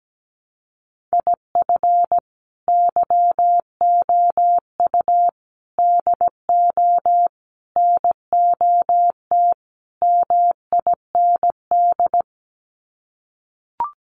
Morse code, heard as a keyed tone, about 17 wpm, 700 Hz. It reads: IF YOU DO NOT MIND
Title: Sets of 5 Words 17wpm